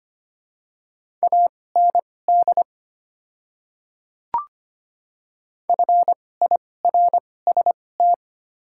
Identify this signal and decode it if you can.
Morse code, heard as a keyed tone, about 25 wpm, spaced as for 18 wpm, 700 Hz. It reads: AND FIRST
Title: 2x Character Spacing - Top 300 Words - 25wpm